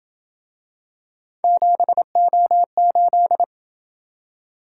Morse code, heard as a keyed tone, about 27 wpm, 700 Hz. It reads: 7O8